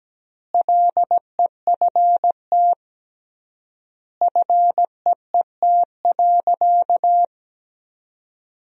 Morse code, heard as a keyed tone, about 17 wpm, 700 Hz. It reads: LEFT FEET.
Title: 3x Word-Spacing - Sentences from Top 300 Words - 17wpm